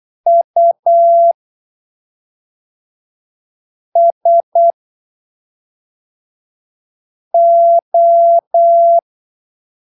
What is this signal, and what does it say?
Morse code, heard as a keyed tone, about 8 wpm, spaced as for 3 wpm, 680 Hz. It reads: USO
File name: arquivos / texto / texto 18 08ppm